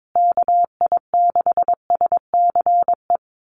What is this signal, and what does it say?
Morse code, heard as a keyed tone, about 22 wpm, 700 Hz. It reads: KI6SCE
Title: USA Call Signs Repeated 2x 22wpm